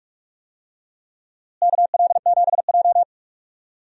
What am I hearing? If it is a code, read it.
Morse code, heard as a keyed tone, about 48 wpm, 700 Hz. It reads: KL7J
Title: DXCC-100 Call Signs - 4 Characters - Repeated 3x - 45wpm